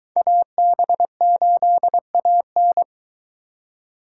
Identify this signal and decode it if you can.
Morse code, heard as a keyed tone, about 23 wpm, 700 Hz. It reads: AB8AN